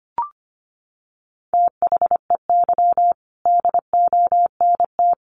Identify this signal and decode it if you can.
Morse code, heard as a keyed tone, about 25 wpm, 700 Hz. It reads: THEY DONT
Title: Sentences from Top 300 Words 25wpm